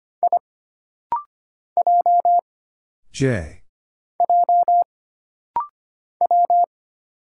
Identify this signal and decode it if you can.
Morse code, heard as a keyed tone, about 25 wpm, 700 Hz. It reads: I J J W
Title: Single Letter-Number Alt - Rapid-Fire 25wpm